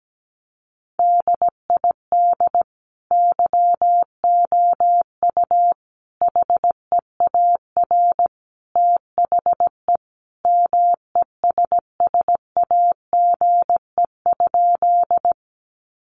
Morse code, heard as a keyed tone, about 17 wpm, 700 Hz. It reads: DID YOU HEAR THE MESSAGE?
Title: Sentences from Top 1100 Words 17wpm